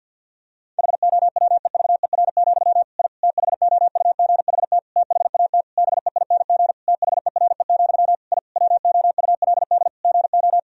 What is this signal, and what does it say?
Morse code, heard as a keyed tone, about 50 wpm, 700 Hz. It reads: HOWEVER, I THOUGHT THAT BEING THERE, I WOULD GO
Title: Sentences from Top 1000 Words - Review Part 3 - 50wpm